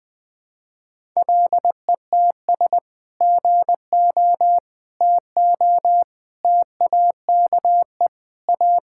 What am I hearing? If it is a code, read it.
Morse code, heard as a keyed tone, about 21 wpm, 700 Hz. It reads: LETS GO TO TAKE A
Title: Sentences with 3-Word Phrases - Spoken Beginning - 20wpm